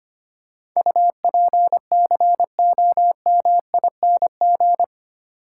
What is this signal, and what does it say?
Morse code, heard as a keyed tone, about 25 wpm, 700 Hz. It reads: UPCOMING
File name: Eight Letter Words - 25wpm